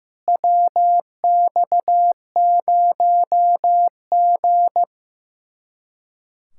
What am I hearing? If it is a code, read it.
Morse code, heard as a keyed tone, about 15 wpm, 700 Hz. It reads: WX0G
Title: Speed-Racing Common QSO Phrases Encore - 15wpm